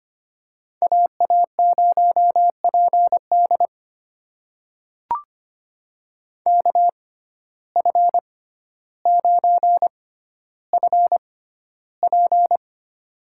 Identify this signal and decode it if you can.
Morse code, heard as a keyed tone, about 25 wpm, spaced as for 13 wpm, 700 Hz. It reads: AA0PD KF9FP
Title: USA Call Signs - 5 Characters - 6x Character Spacing - 25wpm